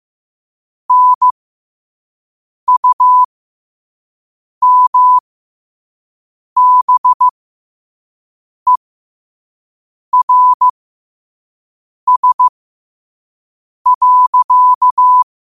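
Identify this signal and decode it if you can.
Morse code, heard as a keyed tone, about 15 wpm, spaced as for 5 wpm, 1000 Hz. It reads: NUMBERS.
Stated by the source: Morse code sent at 5 wpm for learning.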